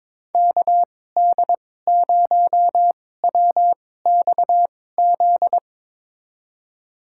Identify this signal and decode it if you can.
Morse code, heard as a keyed tone, about 21 wpm, spaced as for 16 wpm, 700 Hz. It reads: KD0WXZ
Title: USA Call Signs - 6 Characters - 2x Character Spacing - 22wpm